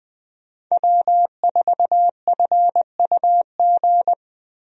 Morse code, heard as a keyed tone, about 20 wpm, 700 Hz. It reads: W4FUG
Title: USA Call Signs - 5 Characters - 20wpm